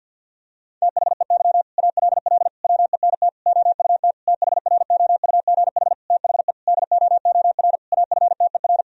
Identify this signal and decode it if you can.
Morse code, heard as a keyed tone, about 53 wpm, 700 Hz. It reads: THEY ALL WENT OUT THROUGH THE DOOR AFTER
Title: Sentences from Top 1000 Words - Review Part 1 - 50wpm